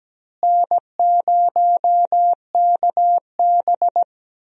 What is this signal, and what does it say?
Morse code, heard as a keyed tone, about 17 wpm, 700 Hz. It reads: N0KB
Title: USA Call Signs - 4 Characters - Repeated 3x - 17wpm